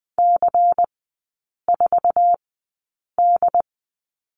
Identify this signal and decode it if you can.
Morse code, heard as a keyed tone, about 20 wpm, spaced as for 8 wpm, 700 Hz. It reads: C4D
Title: 172 Lesson 28e - Number 9 - Three Characters 20wpm